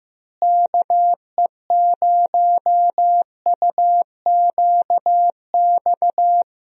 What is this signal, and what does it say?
Morse code, heard as a keyed tone, about 16 wpm, 700 Hz. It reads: KE0UQX
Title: USA Call Signs - 6 Characters - Repeated 3x - 15wpm